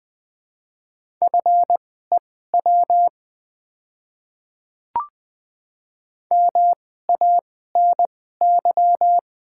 Morse code, heard as a keyed tone, about 20 wpm, spaced as for 14 wpm, 700 Hz. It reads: FEW MANY
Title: 2x Character Spacing - Top 300 Words - 20wpm